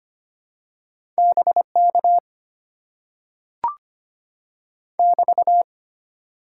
Morse code, heard as a keyed tone, about 25 wpm, 700 Hz. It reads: BK <BT>
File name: Top 90 QSO Elements - 6x Character Spacing - 25wpm